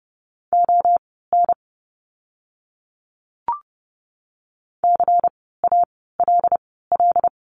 Morse code, heard as a keyed tone, about 30 wpm, spaced as for 17 wpm, 700 Hz. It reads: ON CALL
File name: course / 30:10wpm / 230 Lesson 37f - Letter x - Words 30wpm